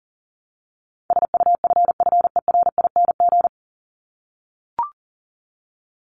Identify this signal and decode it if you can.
Morse code, heard as a keyed tone, about 40 wpm, 700 Hz. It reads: SUFFERING
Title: ING Suffix - Top 100 Words with Base Words 40wpm